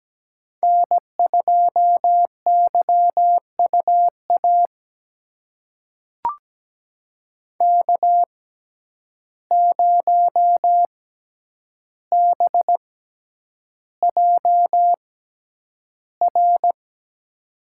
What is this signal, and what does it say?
Morse code, heard as a keyed tone, about 17 wpm, spaced as for 9 wpm, 700 Hz. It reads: N2YUA K0BJR
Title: USA Call Signs - 5 Characters - 6x Character Spacing - 17wpm